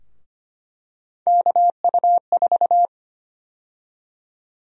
Morse code, heard as a keyed tone, about 25 wpm, 700 Hz. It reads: KU4
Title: USA Call Sign Prefixes - 25wpm